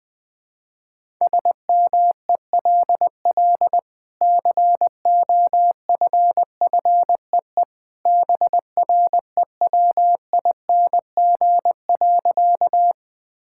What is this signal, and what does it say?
Morse code, heard as a keyed tone, about 21 wpm, 700 Hz. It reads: SMELL COFFEE BREWING.
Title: Easy Sentences with Hints - 20wpm